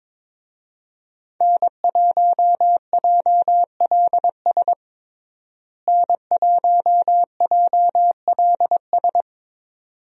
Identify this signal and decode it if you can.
Morse code, heard as a keyed tone, about 21 wpm, 700 Hz. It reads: N1JLS N1JLS
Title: USA Call Signs - 5 Characters - Repeated 2x - 22wpm